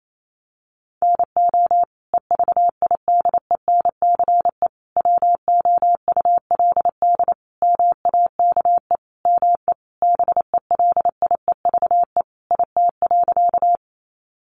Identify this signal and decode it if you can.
Morse code, heard as a keyed tone, about 28 wpm, 700 Hz. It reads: NO EVIDENCE WOULD MAKE ME BELIEVE IT.
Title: Sentences from Top 1300 Words 28wpm